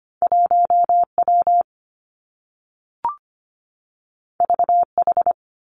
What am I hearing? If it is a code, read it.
Morse code, heard as a keyed tone, about 25 wpm, 700 Hz. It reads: 1 W V H
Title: Two Challenging Letter-Number Combinations 25wpm